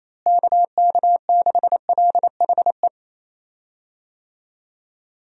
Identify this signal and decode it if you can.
Morse code, heard as a keyed tone, about 28 wpm, 700 Hz. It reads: KK6LHE
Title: USA Call Signs - 6 Characters - Repeated 2x - 28wpm